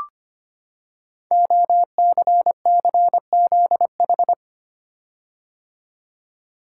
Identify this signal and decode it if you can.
Morse code, heard as a keyed tone, about 25 wpm, 700 Hz. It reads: OCCZH
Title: Code Groups - 5 Letters - 25wpm